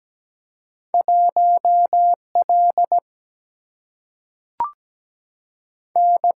Morse code, heard as a keyed tone, about 17 wpm, 700 Hz. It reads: 1L N
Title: Single-Letter Single-Number 17wpm